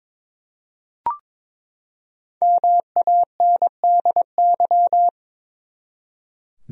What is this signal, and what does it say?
Morse code, heard as a keyed tone, about 22 wpm, 700 Hz. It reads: MANDY
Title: USA Common Names - Max 5 Letters - 22wpm